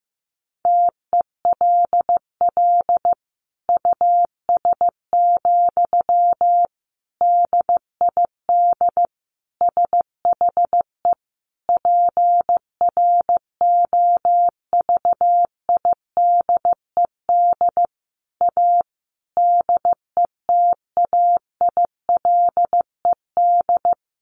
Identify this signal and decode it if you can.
Morse code, heard as a keyed tone, about 14 wpm, 700 Hz. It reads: TELL US, DID SHE PROVIDED A DETAILED
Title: Sentences from Top 1500 Words 15wpm